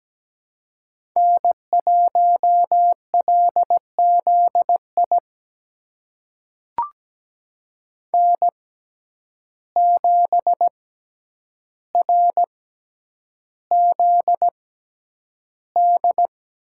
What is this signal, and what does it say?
Morse code, heard as a keyed tone, about 17 wpm, spaced as for 9 wpm, 700 Hz. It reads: N1LZI N7RZD
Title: USA Call Signs - 5 Characters - 6x Character Spacing - 17wpm